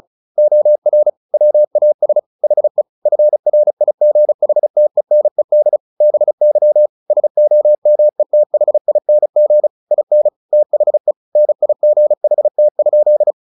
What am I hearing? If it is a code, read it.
Morse code, heard as a keyed tone, about 34 wpm, 600 Hz. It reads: OR WAS HE FRIGHTENED BY SOMETHING IN THE NIGHT?
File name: the old man and the sea 2 35wpm